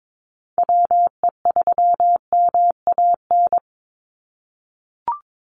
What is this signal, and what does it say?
Morse code, heard as a keyed tone, about 22 wpm, 700 Hz. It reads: WE3MAN
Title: USA Call Signs 22wpm